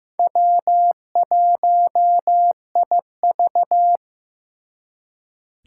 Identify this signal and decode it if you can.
Morse code, heard as a keyed tone, about 15 wpm, 700 Hz. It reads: W1IV
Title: Four Challenging Letter-Number Combinations 15wpm